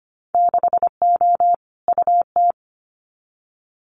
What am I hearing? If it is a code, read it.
Morse code, heard as a keyed tone, about 25 wpm, 700 Hz. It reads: 6O UT
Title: Winter Field Day Exchanges Encore 25wpm